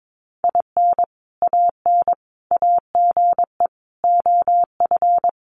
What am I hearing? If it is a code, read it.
Morse code, heard as a keyed tone, about 22 wpm, 700 Hz. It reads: IN AN AGE OF
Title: Sentences from Top 1200 Words 22wpm